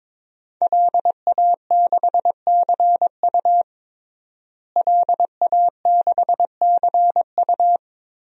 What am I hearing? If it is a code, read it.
Morse code, heard as a keyed tone, about 22 wpm, 700 Hz. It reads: LA6CU LA6CU
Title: DXCC-100 Call Signs - 5 Characters - Repeated 3x - 22wpm